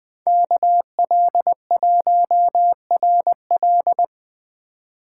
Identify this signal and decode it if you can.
Morse code, heard as a keyed tone, about 20 wpm, 700 Hz. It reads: KL1RL
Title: Speed Racing - USA Call Signs 17wpm